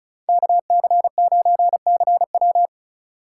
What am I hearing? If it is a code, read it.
Morse code, heard as a keyed tone, about 36 wpm, 700 Hz. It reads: KC9CW
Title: USA Call Signs - 5 Characters - 2x Character Spacing - 35wpm